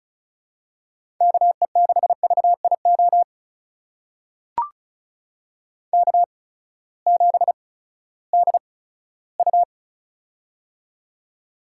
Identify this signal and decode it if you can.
Morse code, heard as a keyed tone, about 36 wpm, spaced as for 18 wpm, 700 Hz. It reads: KE6VIO K7DU
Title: USA Call Signs - 8x Character Spacing - 35wpm